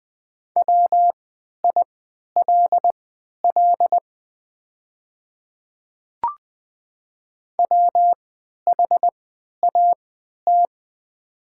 Morse code, heard as a keyed tone, about 20 wpm, spaced as for 11 wpm, 700 Hz. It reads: WILL WHAT
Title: 3x Character Spacing - Top 100 Words - 20wpm